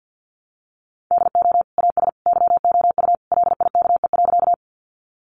Morse code, heard as a keyed tone, about 50 wpm, 700 Hz. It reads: DO AS YOU LIKE.
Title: Sentences from Top 100 Words 50wpm